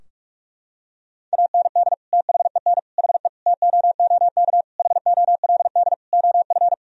Morse code, heard as a keyed tone, about 46 wpm, 700 Hz. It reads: AND THEN HE TOOK HOLD OF
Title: Sentences from Top 1000 Words - Review Part 2 - 45wpm